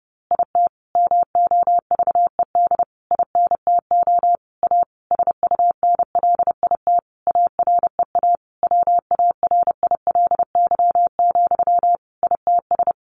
Morse code, heard as a keyed tone, about 30 wpm, 700 Hz. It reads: IT MOVED INTO A SUNLIT AREA WARILY, ITS
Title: A Sound Of Thunder 30wpm